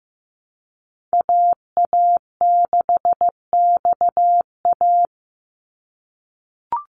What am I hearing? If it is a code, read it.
Morse code, heard as a keyed tone, about 15 wpm, 700 Hz. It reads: AA6XA
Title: NA SOTA Activator Callsigns 15wpm